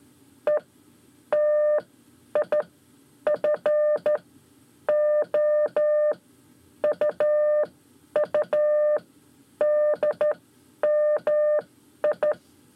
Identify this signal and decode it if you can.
Morse code, heard as a keyed tone, about 12 wpm, spaced as for 9 wpm, 600 Hz. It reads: ETIFO UUDMI
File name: 5 Lezione CW 1